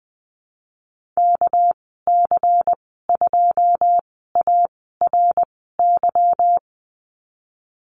Morse code, heard as a keyed tone, about 20 wpm, spaced as for 14 wpm, 700 Hz. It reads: KC2ARY
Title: 200 Lesson 32g - Letter k - Call Signs 20wpm